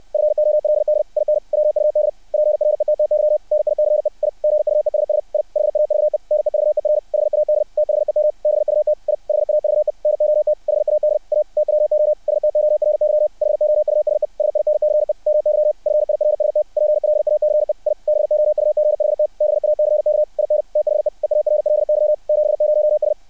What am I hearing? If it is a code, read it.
Morse code, heard as a keyed tone, about 60 wpm, 600 Hz. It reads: FOLD IN WUD PRETTY NICE I GUESS I HAVE NEVER HAD THEM BUT I HAVE NOT HAD A TWO STORY HOUSE SINCE MY FIRST PLACE I BOUGHT BACK IN THE EARLY 70S
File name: archived / CRSnet